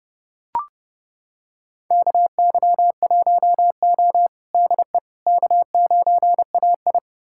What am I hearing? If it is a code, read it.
Morse code, heard as a keyed tone, about 30 wpm, 700 Hz. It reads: KY1O DE K9AI
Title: USA Callsign DE Callsign - 4 Characters - 30wpm